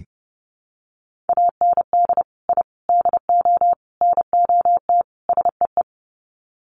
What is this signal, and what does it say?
Morse code, heard as a keyed tone, about 30 wpm, 700 Hz. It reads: AND I DO NOT SEE
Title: Sets of 5 Words Encore2 30wpm